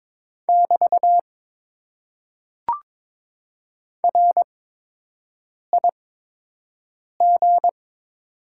Morse code, heard as a keyed tone, about 23 wpm, spaced as for 6 wpm, 690 Hz. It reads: <BT> RIG
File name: Top 90 QSO Elements - 8x Character Spacing - 22wpm